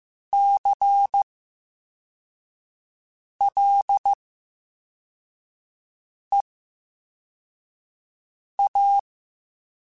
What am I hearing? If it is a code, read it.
Morse code, heard as a keyed tone, about 15 wpm, spaced as for 4 wpm, 800 Hz. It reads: CLEA
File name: Special INGs / Special INGs-mp3s / Slow ing @13